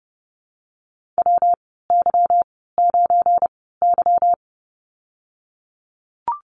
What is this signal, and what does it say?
Morse code, heard as a keyed tone, about 30 wpm, spaced as for 17 wpm, 700 Hz. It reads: WY9Y
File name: Lesson 37g - Letter x - Call Signs 30wpm